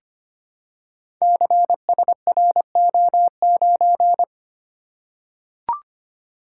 Morse code, heard as a keyed tone, about 25 wpm, 700 Hz. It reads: CSRO9
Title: Code Groups - 5 Letter:Numbers - 25wpm